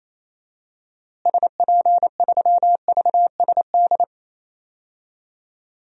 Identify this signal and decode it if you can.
Morse code, heard as a keyed tone, about 28 wpm, 700 Hz. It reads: SP3VSD
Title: DXCC-100 Call Signs - 6 Characters - Repeated 3x - 28wpm